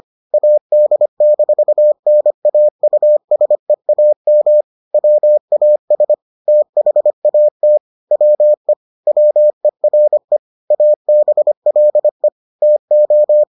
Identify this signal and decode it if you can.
Morse code, heard as a keyed tone, about 25 wpm, 600 Hz. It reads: AD-NAUSEAM WAS THAT WE WERE ABLE TO